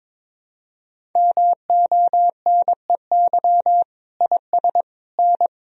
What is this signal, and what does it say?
Morse code, heard as a keyed tone, about 22 wpm, 700 Hz. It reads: MONEY IS N
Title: Sentences from Top 500 Words and Abbreviations 22wpm